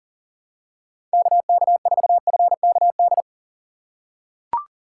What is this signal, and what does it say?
Morse code, heard as a keyed tone, about 40 wpm, 700 Hz. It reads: KK4FKD
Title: USA Call Signs - 6 Characters - 3x Character Spacing - 40wpm